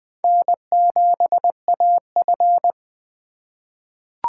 Morse code, heard as a keyed tone, about 20 wpm, 700 Hz. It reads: N7AF
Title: USA Call Signs - 3x Character Spacing - 20wpm